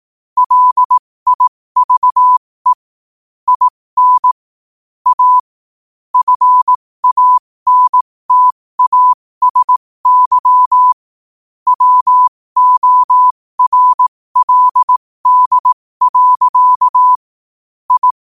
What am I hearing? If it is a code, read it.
Morse code, heard as a keyed tone, about 18 wpm, spaced as for 15 wpm, 1000 Hz. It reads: LIVE IN A FANTASY WORLD. I
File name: qotdfiles / 15wpm